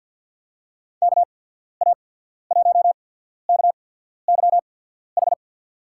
Morse code, heard as a keyed tone, about 47 wpm, spaced as for 13 wpm, 700 Hz. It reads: KA1KYH
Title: USA Call Signs - 6 Characters - 8x Character Spacing - 50wpm